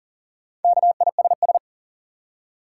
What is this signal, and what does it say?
Morse code, heard as a keyed tone, about 40 wpm, 700 Hz. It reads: KISS
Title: Four Letter Words - 40wpm